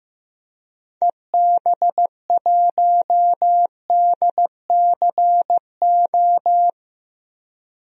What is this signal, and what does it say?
Morse code, heard as a keyed tone, about 15 wpm, 700 Hz. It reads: EB1DCO
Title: DXCC-100 Call Signs - 6 Characters - Repeated 3x - 15wpm